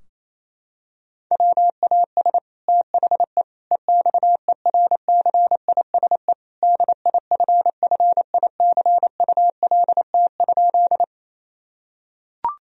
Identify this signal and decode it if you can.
Morse code, heard as a keyed tone, about 28 wpm, 700 Hz. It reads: WAS THE EXERCISE DIFFICULT?
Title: Sentences from Top 1000 Words - Review Part 4 - 28wpm